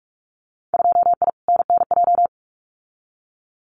Morse code, heard as a keyed tone, about 44 wpm, 700 Hz. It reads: 2I NNJ
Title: Winter Field Day Exchanges Encore 45wpm